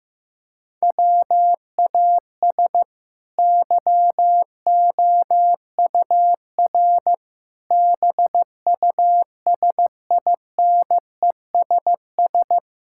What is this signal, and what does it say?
Morse code, heard as a keyed tone, about 15 wpm, 700 Hz. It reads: WAS YOUR BUSINESS
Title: Sentences from Top 1000 Words - Review Part 5 - 15wpm